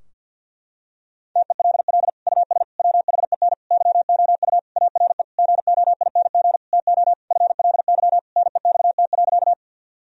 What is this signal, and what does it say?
Morse code, heard as a keyed tone, about 52 wpm, 700 Hz. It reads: TELL US WHEN YOU ARE GOING TO FLY NEXT.
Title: Sentences from Top 1000 Words - Review Part 2 - 50wpm